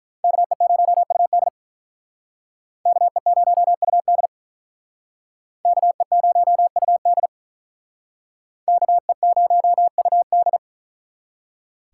Morse code, heard as a keyed tone, about 44 wpm, 700 Hz. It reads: KE0UD KE0UD KE0UD KE0UD